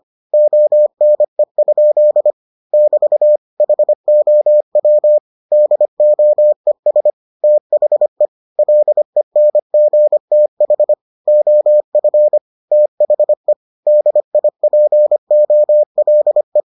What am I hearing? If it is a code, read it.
Morse code, heard as a keyed tone, about 25 wpm, 600 Hz. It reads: ONE? <BT> HOW DOES THE LENGTH OF THE DIPOLE